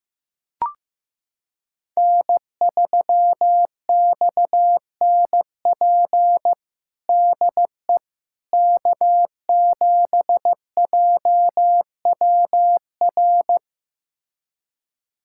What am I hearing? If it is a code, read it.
Morse code, heard as a keyed tone, about 15 wpm, 700 Hz. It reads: N3XNP DE K7JWR
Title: USA Callsign DE Callsign - 5 Characters - 15wpm